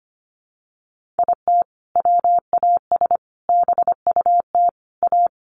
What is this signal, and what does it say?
Morse code, heard as a keyed tone, about 25 wpm, 700 Hz. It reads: IT WAS BUT A
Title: Sentences from Top 1500 Words 25wpm